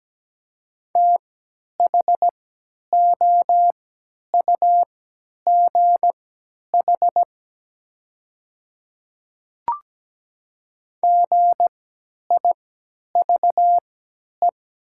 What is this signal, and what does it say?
Morse code, heard as a keyed tone, about 17 wpm, spaced as for 10 wpm, 700 Hz. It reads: THOUGH GIVE
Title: 3x Character Spacing - Top 500 Words - 17wpm